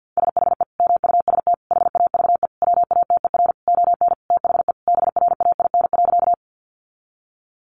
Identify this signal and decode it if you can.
Morse code, heard as a keyed tone, about 50 wpm, 700 Hz. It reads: SHE MUST HAVE WATER ON THE BRAIN.
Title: Sentences from Top 1100 Words 50wpm